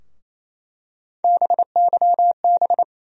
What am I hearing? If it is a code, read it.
Morse code, heard as a keyed tone, about 28 wpm, 700 Hz. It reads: BYB
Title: USA Call Sign Postfixes - 3 Letters 28wpm